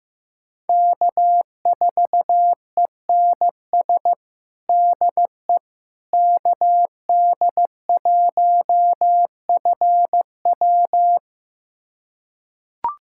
Morse code, heard as a keyed tone, about 16 wpm, 700 Hz. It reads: K4ENS DE KD1FW
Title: USA Callsign DE Callsign 15wpm